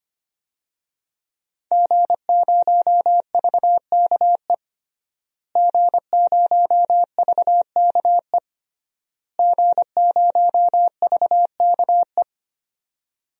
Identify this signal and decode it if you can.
Morse code, heard as a keyed tone, about 25 wpm, 700 Hz. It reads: G0VKE G0VKE G0VKE
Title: DXCC-100 Call Signs - 5 Characters - Repeated 3x - 25wpm